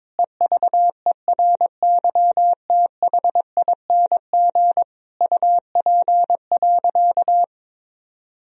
Morse code, heard as a keyed tone, about 21 wpm, 700 Hz. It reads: EVERYTHING UP.
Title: Sentences from Top 1000 Words - Review Part 1 - 22wpm